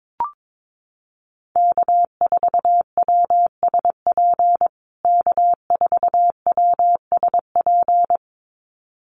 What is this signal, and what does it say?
Morse code, heard as a keyed tone, about 22 wpm, 700 Hz. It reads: K4WSP K4WSP
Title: USA Call Signs Repeated 2x 22wpm